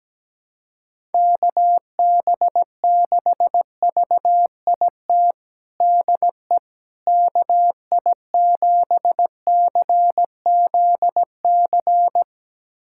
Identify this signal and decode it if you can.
Morse code, heard as a keyed tone, about 17 wpm, 700 Hz. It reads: KB6VIT DE KI7CZC
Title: USA Callsign DE Callsign - 6 Characters - 17wpm